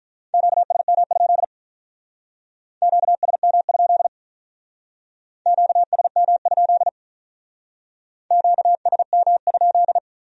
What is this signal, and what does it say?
Morse code, heard as a keyed tone, about 44 wpm, 700 Hz. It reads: QSM? QSM? QSM? QSM?